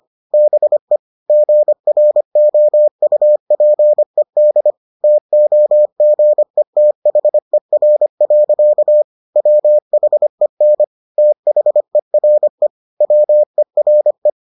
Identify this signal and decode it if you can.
Morse code, heard as a keyed tone, about 25 wpm, 600 Hz. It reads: BE GROUPED TOGETHER. WHEN THERE WERE